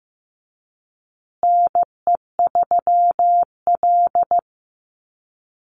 Morse code, heard as a keyed tone, about 15 wpm, 700 Hz. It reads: NE3L
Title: USA Call Signs Repeated 3x 15wpm